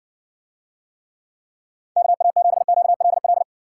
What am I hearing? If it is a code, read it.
Morse code, heard as a keyed tone, about 70 wpm, 700 Hz. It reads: KA7QLL